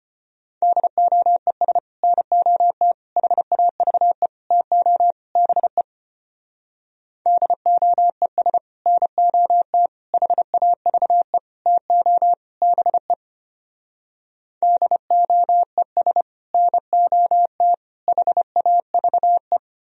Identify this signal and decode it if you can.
Morse code, heard as a keyed tone, about 32 wpm, 700 Hz. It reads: DOES NOT HAVE TO BE DOES NOT HAVE TO BE DOES NOT HAVE